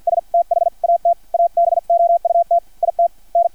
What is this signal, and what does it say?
Morse code, heard as a keyed tone, about 50 wpm, spaced as for 43 wpm, 680 Hz. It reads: S THAT ABOUT IT N
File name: archived / CRSnet